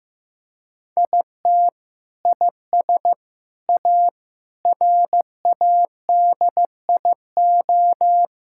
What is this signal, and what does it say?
Morse code, heard as a keyed tone, about 15 wpm, 700 Hz. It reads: IT IS A RADIO
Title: Sentences from Top 1000 Words - Review Part 5 - 15wpm